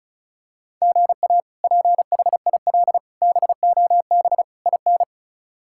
Morse code, heard as a keyed tone, about 34 wpm, 700 Hz. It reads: GA PHIL BOB IN
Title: K1USN SST QSO Practice - 35wpm